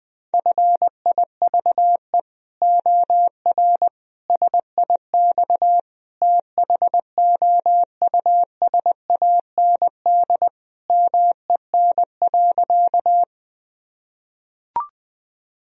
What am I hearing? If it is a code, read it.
Morse code, heard as a keyed tone, about 20 wpm, 700 Hz. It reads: FIVE OR SIX THOUSAND MEN.
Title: Sentences from Top 1000 Words - Review Part 3 - 20wpm